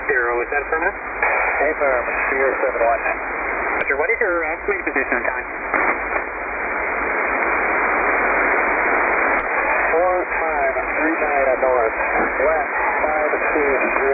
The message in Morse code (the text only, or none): none